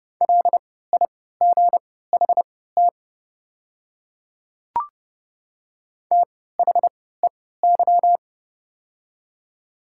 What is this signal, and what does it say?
Morse code, heard as a keyed tone, about 29 wpm, spaced as for 17 wpm, 700 Hz. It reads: LIGHT THEY
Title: 3x Character Spacing - Top 200 Words - 30wpm